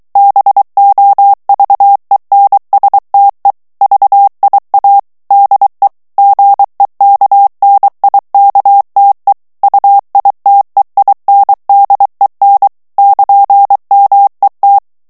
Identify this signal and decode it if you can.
Morse code, heard as a keyed tone, about 23 wpm, 800 Hz. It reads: BOVENSTE VIA DE GEKNIKTE UITEINDEN <KN>MET
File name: HOLLANDAIS texte clair 1 vitesse 22 mots minute